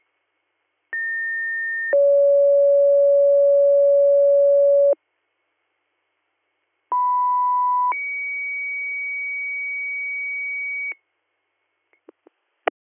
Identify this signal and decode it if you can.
Two-tone page, heard as a one-second tone, then a three-second tone, about 1800 then 560 Hz, about 990 then 2250 Hz. It is heard from 0.9 to 10.9 s.